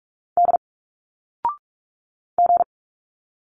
Morse code, heard as a keyed tone, about 46 wpm, 700 Hz. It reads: D G
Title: Single Letters 45wpm